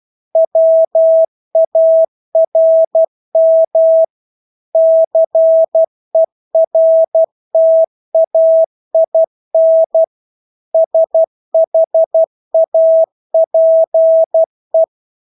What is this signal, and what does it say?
Morse code, heard as a keyed tone, about 12 wpm, 650 Hz. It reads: WARM CERTAIN SHAPE
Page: Engelske ord 12 wpm